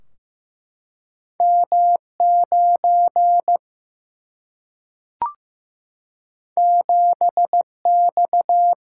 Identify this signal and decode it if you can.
Morse code, heard as a keyed tone, about 15 wpm, 700 Hz. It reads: M 9 7 X
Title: Single-Letter Single-Number 15wpm